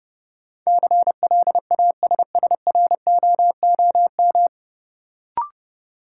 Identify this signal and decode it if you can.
Morse code, heard as a keyed tone, about 30 wpm, 700 Hz. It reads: CLASSROOM
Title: Sentences from Top 1500 Words - New Word Drill Alt - 30wpm